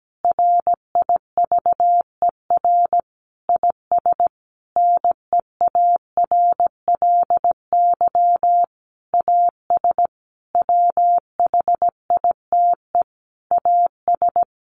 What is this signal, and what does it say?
Morse code, heard as a keyed tone, about 17 wpm, 700 Hz. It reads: RIVER IS NEARLY AS WHITE AS
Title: Sentences from Top 1500 Words 17wpm